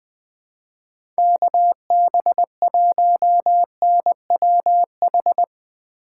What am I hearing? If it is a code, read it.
Morse code, heard as a keyed tone, about 20 wpm, 700 Hz. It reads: KB1NWH
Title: USA Call Signs - 6 Characters - 4x Character Spacing - 20wpm